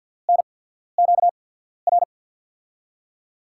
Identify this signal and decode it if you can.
Morse code, heard as a keyed tone, about 51 wpm, spaced as for 14 wpm, 700 Hz. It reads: NQR
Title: USA Call Signs - 6 Characters - 8x Character Spacing - 50wpm